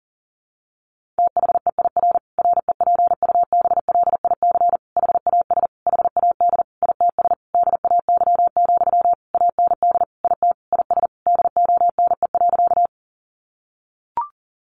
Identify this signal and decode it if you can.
Morse code, heard as a keyed tone, about 40 wpm, 700 Hz. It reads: THEIR REPUBLIC HAS HAD ITS DAY, AND IT IS DONE.
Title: Sentences from Top 1500 Words 40wpm